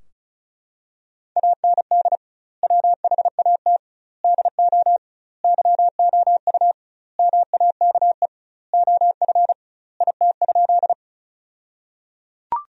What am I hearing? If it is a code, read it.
Morse code, heard as a keyed tone, about 34 wpm, 700 Hz. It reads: AND WHAT DO YOU MAKE OF IT?
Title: 2x Word-Spacing - Sentences from Top 100 Words - 35wpm